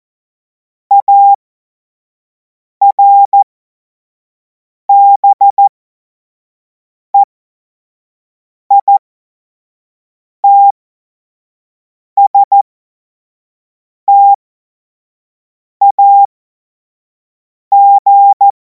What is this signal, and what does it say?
Morse code, heard as a keyed tone, about 14 wpm, spaced as for 5 wpm, 800 Hz. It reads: ARBEITSTAG
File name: Morsekurs / Witze / Witz 31. März 2026 25 F70